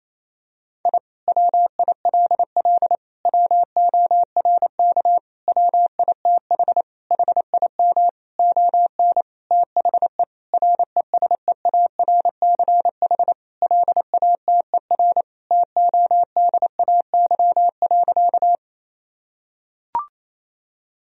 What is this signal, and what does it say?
Morse code, heard as a keyed tone, about 28 wpm, 700 Hz. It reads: I WILL WORK WITH HIM ON THE RESEARCH LATER TODAY.
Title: Sentences from Top 1000 Words - Review Part 4 - 28wpm